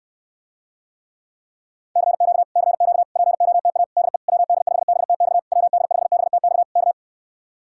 Morse code, heard as a keyed tone, about 69 wpm, 700 Hz. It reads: CQ CQ POTA DE WD5BTX WD5BTX K